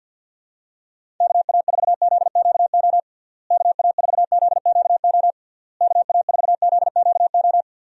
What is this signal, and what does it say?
Morse code, heard as a keyed tone, about 50 wpm, 700 Hz. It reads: KA4ZQO KA4ZQO KA4ZQO
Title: USA Call Signs - 6 Characters - Repeated 3x - 50wpm